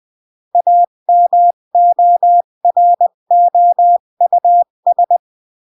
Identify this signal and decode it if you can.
Morse code, heard as a keyed tone, about 20 wpm, 700 Hz. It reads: AMOROUS
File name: CW / jodorowsky / 82 commandments 20-15